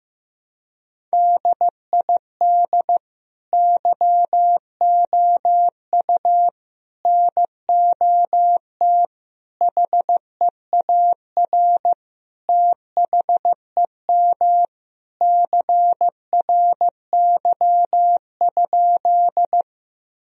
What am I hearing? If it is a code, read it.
Morse code, heard as a keyed tone, about 15 wpm, 700 Hz. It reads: DID YOU NOT HEAR THEM CRY?
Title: Sentences from Top 1000 Words - Review Part 2 - 15wpm